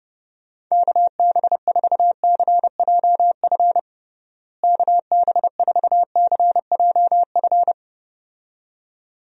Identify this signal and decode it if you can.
Morse code, heard as a keyed tone, about 32 wpm, 700 Hz. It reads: KB4CJF KB4CJF
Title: USA Call Signs - 6 Characters - Repeated 2x - 30wpm